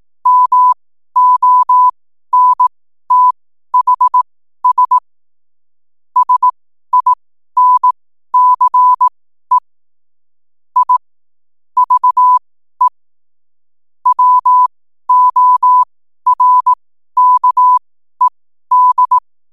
Morse code, heard as a keyed tone, about 18 wpm, spaced as for 12 wpm, 1000 Hz. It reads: MONTHS SINCE I VE WORKED
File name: n7net / 12wpm